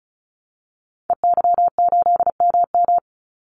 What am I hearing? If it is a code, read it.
Morse code, heard as a keyed tone, about 34 wpm, 700 Hz. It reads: EY8MM
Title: Call Signs Repeated 3x - 35wpm